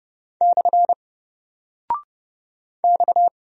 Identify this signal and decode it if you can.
Morse code, heard as a keyed tone, about 30 wpm, 700 Hz. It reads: / X
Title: Single Character (Letter:Number:Punctuation:Prosign) 30wpm